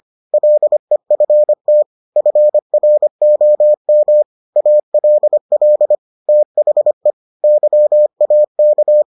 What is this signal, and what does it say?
Morse code, heard as a keyed tone, about 25 wpm, 600 Hz. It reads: LEFT FROM ALL THE YAK